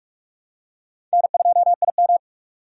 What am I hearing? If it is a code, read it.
Morse code, heard as a keyed tone, about 44 wpm, 700 Hz. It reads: N2IM
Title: USA Call Signs - 4 Characters - 45wpm